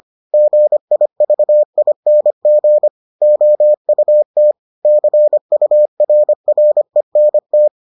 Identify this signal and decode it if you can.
Morse code, heard as a keyed tone, about 25 wpm, 600 Hz. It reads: GIVING OUT CURRENT